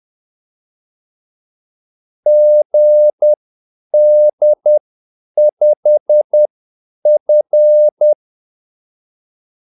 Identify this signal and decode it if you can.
Morse code, heard as a keyed tone, about 10 wpm, spaced as for 8 wpm, 600 Hz. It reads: GD5F
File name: Callsigns-10-8wpm